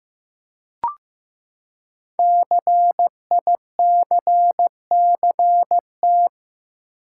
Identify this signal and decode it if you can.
Morse code, heard as a keyed tone, about 15 wpm, 700 Hz. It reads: CICCT
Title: Code Groups - 5 Letters - 15wpm